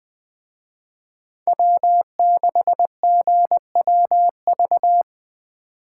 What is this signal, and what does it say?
Morse code, heard as a keyed tone, about 20 wpm, 700 Hz. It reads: W6GWV
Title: Speed Racing - USA 1x3 Call Signs 17wpm